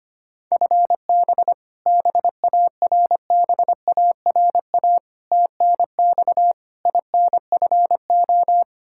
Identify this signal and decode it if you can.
Morse code, heard as a keyed tone, about 25 wpm, 700 Hz. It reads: FB BARBARA TNX INFO
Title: Practice QSO - Exchange 3 - More Info - 25wpm